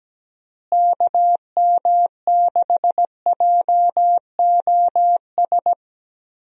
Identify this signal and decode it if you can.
Morse code, heard as a keyed tone, about 17 wpm, 700 Hz. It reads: KM6JOS
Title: USA Call Signs - 6 Characters - Repeated 3x - 17wpm